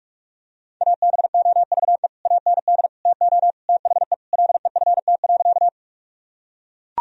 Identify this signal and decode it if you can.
Morse code, heard as a keyed tone, about 41 wpm, 700 Hz. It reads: ABOVE AND TO THE LEFT.
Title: Sentences from Top 1000 Words - Review Part 2 - 45wpm